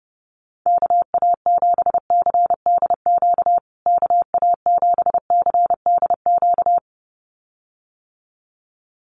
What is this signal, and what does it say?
Morse code, heard as a keyed tone, about 30 wpm, 700 Hz. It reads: KA7CDQ KA7CDQ
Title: USA Call Signs Repeated 2x 30wpm